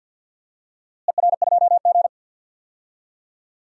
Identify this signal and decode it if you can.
Morse code, heard as a keyed tone, about 52 wpm, 700 Hz. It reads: ER2G
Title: DXCC-100 Call Signs - 4 Characters - Repeated 3x - 50wpm